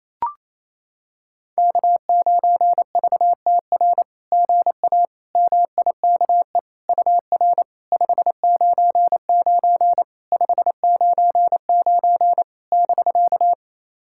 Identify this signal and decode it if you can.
Morse code, heard as a keyed tone, about 28 wpm, 700 Hz. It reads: K9VTR GA MIKE UR 599 599 <BK>
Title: SOTA QSO Practice 28wpm